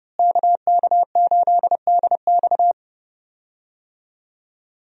Morse code, heard as a keyed tone, about 30 wpm, 700 Hz. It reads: KK8DX
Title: USA Call Signs - 5 Characters - Repeated 2x - 30wpm